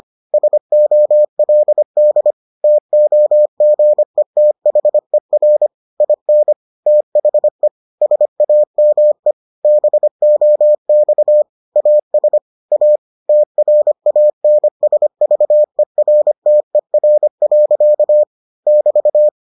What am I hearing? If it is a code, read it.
Morse code, heard as a keyed tone, about 25 wpm, 600 Hz. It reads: SOLD TOGETHER IN THE SAME BOX AS A TRANSVERTER. <BT>